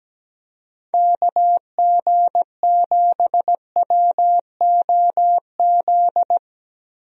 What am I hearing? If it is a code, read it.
Morse code, heard as a keyed tone, about 17 wpm, 700 Hz. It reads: KG7WOZ
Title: USA Call Signs - 6 Characters - 17wpm